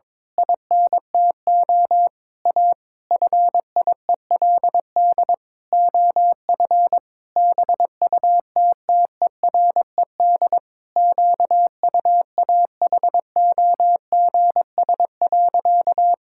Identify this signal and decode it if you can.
Morse code, heard as a keyed tone, about 22 wpm, 700 Hz. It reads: INTO A FIELD OF BUTTERED QUAHOGS.
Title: Pangram Challenge v2 - 22wpm